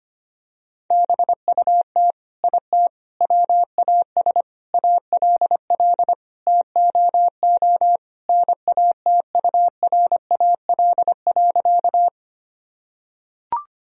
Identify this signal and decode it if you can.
Morse code, heard as a keyed tone, about 25 wpm, 700 Hz. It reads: BUT IT WAS ALL TOO NATURAL.
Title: Sentences from Top 1000 Words - Review Part 4 - 25wpm